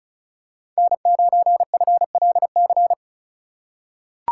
Morse code, heard as a keyed tone, about 34 wpm, 700 Hz. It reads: N9FLC
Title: USA Call Signs - 5 Characters - 2x Character Spacing - 35wpm